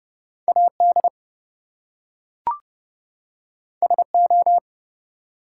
Morse code, heard as a keyed tone, about 30 wpm, 700 Hz. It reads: AD SO
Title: Two Letter Words - 30wpm